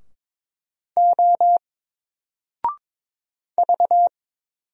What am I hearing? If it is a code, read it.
Morse code, heard as a keyed tone, about 22 wpm, 700 Hz. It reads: O V
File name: Single-Letter-Number-22wpm